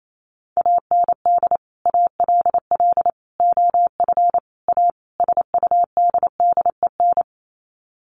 Morse code, heard as a keyed tone, about 28 wpm, 700 Hz. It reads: AND ALL OF A SUDDEN
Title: Sets of 5 Words 28wpm